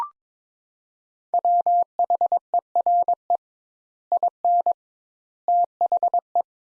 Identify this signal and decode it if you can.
Morse code, heard as a keyed tone, about 22 wpm, 700 Hz. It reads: WHERE IN THE
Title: Sentences Top 200 Words - 22wpm